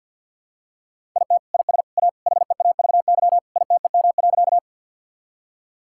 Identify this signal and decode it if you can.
Morse code, heard as a keyed tone, about 51 wpm, 700 Hz. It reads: IT IS A HEAVY ITEM.
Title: Sentences from Top 1000 Words - Review Part 6 - 50wpm